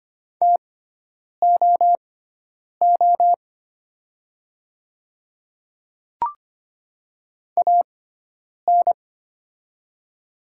Morse code, heard as a keyed tone, about 25 wpm, spaced as for 9 wpm, 700 Hz. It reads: TOO AN
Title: 6x Character Spacing - Top 300 Words - 25wpm